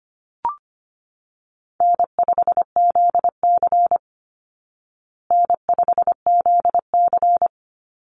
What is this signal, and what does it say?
Morse code, heard as a keyed tone, about 25 wpm, 700 Hz. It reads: N5ZC N5ZC
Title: Easy Call Signs Repeated 2x - 25wpm